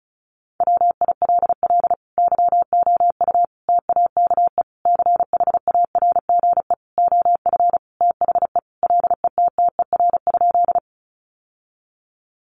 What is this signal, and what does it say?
Morse code, heard as a keyed tone, about 34 wpm, 700 Hz. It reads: WILL YOU TAKE CHARGE OF THE LETTER?
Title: Sentences from Top 900 Words 35wpm